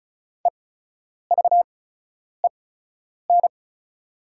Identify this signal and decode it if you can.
Morse code, heard as a keyed tone, about 34 wpm, spaced as for 10 wpm, 700 Hz. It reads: EVEN
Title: 8x Character Spacing - Top 300 Words - 35wpm